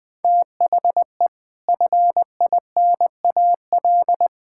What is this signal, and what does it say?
Morse code, heard as a keyed tone, about 19 wpm, 700 Hz. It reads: THE FINAL
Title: Sets of 6 Words - 20wpm